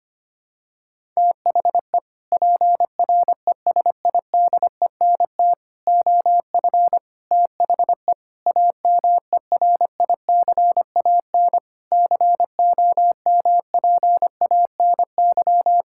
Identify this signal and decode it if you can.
Morse code, heard as a keyed tone, about 25 wpm, 700 Hz. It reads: THE PRESIDENT OF THE AMERICAN COMPANY
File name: practice / 25wpm / Sentences with 3-Word Phrases - 25wpm